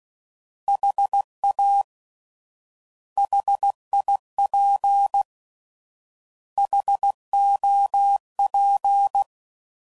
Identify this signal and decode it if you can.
Morse code, heard as a keyed tone, about 16 wpm, 800 Hz. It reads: HA HIP HOP